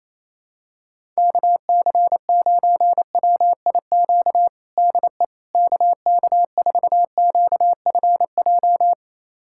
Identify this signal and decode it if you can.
Morse code, heard as a keyed tone, about 27 wpm, 700 Hz. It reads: KC9WIQ DE KK4QFJ
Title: USA Callsign DE Callsign - 6 Characters - 28wpm